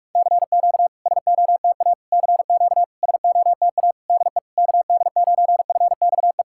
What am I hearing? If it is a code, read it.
Morse code, heard as a keyed tone, about 44 wpm, 690 Hz. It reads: CQ SOTA CQ SOTA DE KD9FXE
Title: SOTA QSO Practice 45wpm